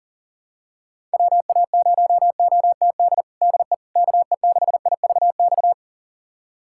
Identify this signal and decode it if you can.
Morse code, heard as a keyed tone, about 40 wpm, 700 Hz. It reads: WA0OTD DE KE6IVX
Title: USA Callsign DE Callsign 40wpm